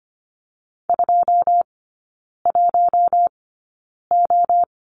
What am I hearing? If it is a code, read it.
Morse code, heard as a keyed tone, about 25 wpm, spaced as for 9 wpm, 700 Hz. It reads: 21O
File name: course / 25:10wpm / 093 Lesson 16e - Letter c - Three Characters 25wpm